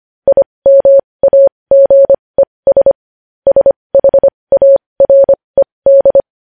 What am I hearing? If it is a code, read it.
Morse code, heard as a keyed tone, about 25 wpm, spaced as for 20 wpm, 550 Hz. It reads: IMAGES SHARED
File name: podcast / News headlines-1x-25x20